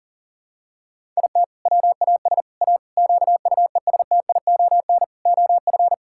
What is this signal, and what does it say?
Morse code, heard as a keyed tone, about 40 wpm, 700 Hz. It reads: IT WAS A QUESTION OF
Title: Sentences from Top 1000 Words - Review Part 6 - 40wpm